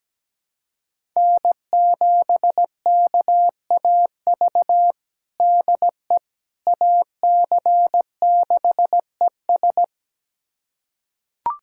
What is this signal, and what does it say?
Morse code, heard as a keyed tone, about 17 wpm, 700 Hz. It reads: N7KAV DE AC6ES
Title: USA Callsign DE Callsign - 5 Characters - 17wpm